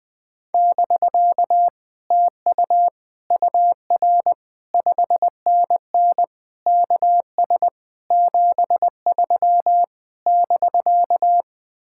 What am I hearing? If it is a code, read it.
Morse code, heard as a keyed tone, about 20 wpm, 700 Hz. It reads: <BK> TU UR 5NN KS 73 <BK>
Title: Speed-Racing Common QSO Phrases Encore - 17wpm